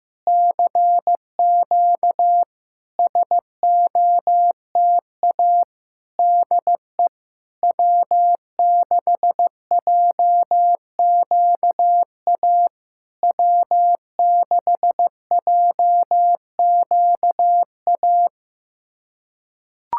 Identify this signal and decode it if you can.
Morse code, heard as a keyed tone, about 15 wpm, 700 Hz. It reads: CQ SOTA DE W6JQA W6JQA
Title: SOTA QSO Practice 15wpm